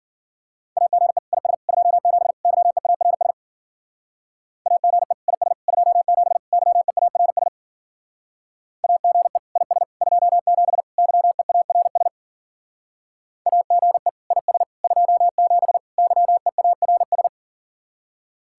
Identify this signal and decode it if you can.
Morse code, heard as a keyed tone, about 49 wpm, 700 Hz. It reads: AGE IS 27 YEARS AGE IS 27 YEARS AGE IS 27 YEARS AGE IS 27 YEARS
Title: Speed-Racing Common QSO Phrases - 40wpm